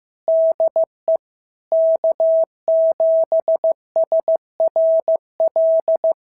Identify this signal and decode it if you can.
Morse code, heard as a keyed tone, about 14 wpm, 650 Hz. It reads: DE K7SRL
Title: SOTA QSO Practice 15wpm